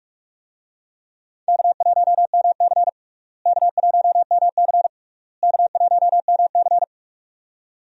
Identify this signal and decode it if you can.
Morse code, heard as a keyed tone, about 41 wpm, 700 Hz. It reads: K1MC K1MC K1MC
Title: USA Call Signs - 4 Characters - Repeated 3x - 45wpm